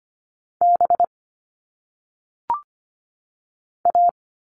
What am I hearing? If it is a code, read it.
Morse code, heard as a keyed tone, about 25 wpm, 700 Hz. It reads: B A
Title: All Cut Numbers - 1-Digit 25wpm